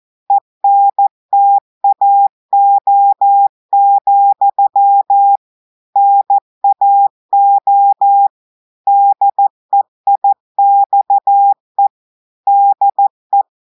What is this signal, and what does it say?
Morse code, heard as a keyed tone, about 14 wpm, 800 Hz. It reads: ENTAO, NAO DEIXE DE
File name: audios / portugais / portugais texte en clair 2 vitesse 14 mots minutes